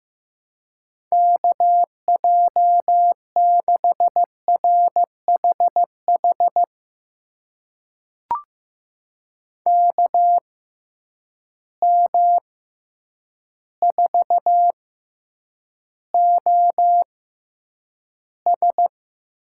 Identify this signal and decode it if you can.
Morse code, heard as a keyed tone, about 15 wpm, spaced as for 8 wpm, 700 Hz. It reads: KJ6RHH KM4OS
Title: USA Call Signs - 6 Characters - 6x Character Spacing - 15wpm